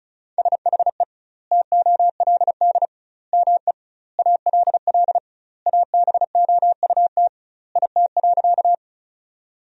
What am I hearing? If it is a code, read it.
Morse code, heard as a keyed tone, about 36 wpm, 700 Hz. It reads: SHE TOLD ME ALL ABOUT IT.
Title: 2x Word-Spacing - Sentences from Top 400 Words - 35wpm